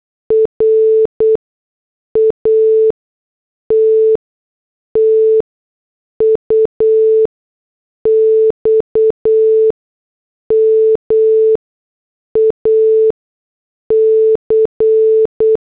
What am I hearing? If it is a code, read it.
Morse code, heard as a keyed tone, about 8 wpm, 440 Hz. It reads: R A T T U X M A C
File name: morse / headlines